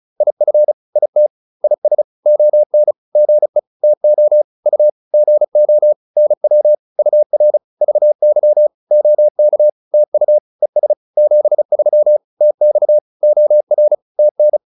Morse code, heard as a keyed tone, about 34 wpm, 600 Hz. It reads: IF IT IS ON GE TO U GO NW UR VY OK TU ES 73 TX OR TN
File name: PR101 35